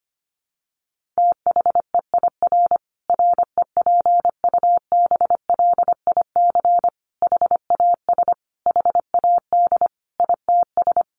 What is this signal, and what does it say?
Morse code, heard as a keyed tone, about 25 wpm, 700 Hz. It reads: THEIR REPUBLIC HAS HAD ITS
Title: Sentences from Top 1500 Words 25wpm